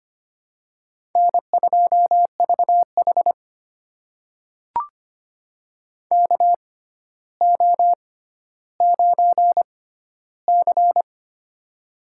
Morse code, heard as a keyed tone, about 25 wpm, spaced as for 13 wpm, 700 Hz. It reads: N2VH KO9C
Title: USA Call Signs - 4 Characters - 6x Character Spacing - 25wpm